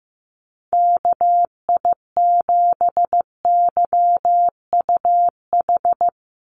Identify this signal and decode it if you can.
Morse code, heard as a keyed tone, about 15 wpm, 700 Hz. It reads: KI7YUH
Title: USA Call Signs 15wpm